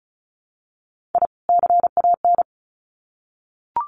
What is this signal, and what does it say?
Morse code, heard as a keyed tone, about 34 wpm, 700 Hz. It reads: I CAN
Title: Sets of 2 Words 35wpm